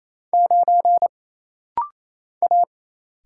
Morse code, heard as a keyed tone, about 27 wpm, 700 Hz. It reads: 9 A
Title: Single Letter-Number - Rapid-Fire 28wpm